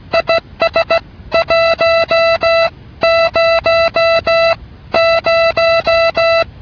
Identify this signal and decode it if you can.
Morse code, heard as a keyed tone, about 16 wpm, 660 Hz. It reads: IS 1 0 0